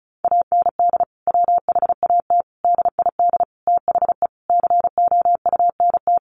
Morse code, heard as a keyed tone, about 34 wpm, 700 Hz. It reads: AND WHAT DID THE COUNT
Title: Sentences from Top 1500 Words 35wpm